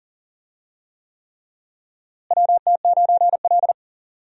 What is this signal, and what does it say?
Morse code, heard as a keyed tone, about 40 wpm, 700 Hz. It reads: WT9L
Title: USA Call Signs - 4 Characters - Repeated 3x - 40wpm